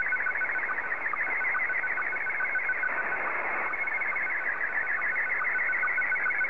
SSTV (slow-scan television), Scottie 1.